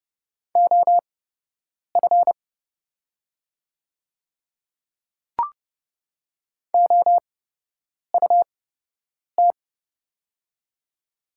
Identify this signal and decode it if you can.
Morse code, heard as a keyed tone, about 29 wpm, spaced as for 8 wpm, 700 Hz. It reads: OF OUT
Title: 8x Character Spacing - Top 100 Words - 30wpm